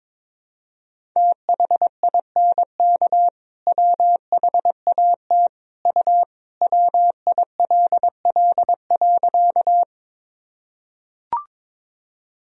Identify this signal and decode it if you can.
Morse code, heard as a keyed tone, about 22 wpm, 700 Hz. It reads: THINK WHAT U WILL.
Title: Sentences from Top 200 Words and Abbreviations 22wpm